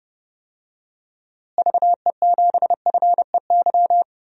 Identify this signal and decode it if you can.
Morse code, heard as a keyed tone, about 30 wpm, 700 Hz. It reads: VE7FEY
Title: DXCC-100 Call Signs - 6 Characters - Repeated 3x - 30wpm